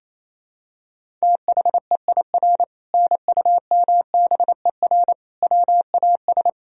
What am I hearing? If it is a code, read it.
Morse code, heard as a keyed tone, about 27 wpm, 700 Hz. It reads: THEIR NUMBER WAS
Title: Sentences from Top 1000 Words - Review Part 1 - 28wpm